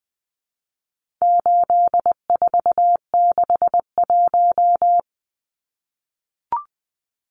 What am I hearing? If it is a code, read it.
Morse code, heard as a keyed tone, about 20 wpm, 700 Hz. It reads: 8461
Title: Numbers - 4-Digit 20wpm